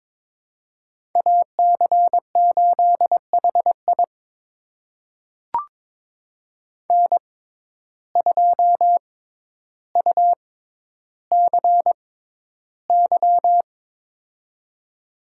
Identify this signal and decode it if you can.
Morse code, heard as a keyed tone, about 22 wpm, spaced as for 11 wpm, 700 Hz. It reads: AC8HI N2UCY
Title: USA Call Signs - 5 Characters - 6x Character Spacing - 22wpm